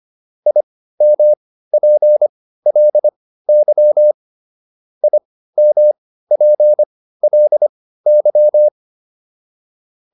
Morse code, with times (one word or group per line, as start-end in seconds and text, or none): 0.5-4.1 IMPLY
5.0-8.7 IMPLY